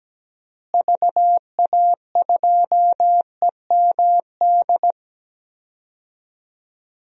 Morse code, heard as a keyed tone, about 17 wpm, 700 Hz. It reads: VA2EMD
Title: DXCC-100 Call Signs - 6 Characters - Repeated 3x - 17wpm